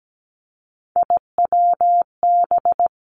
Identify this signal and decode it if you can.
Morse code, heard as a keyed tone, about 17 wpm, 700 Hz. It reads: IWB
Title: Three Challenging Letter-Number Combinations 17wpm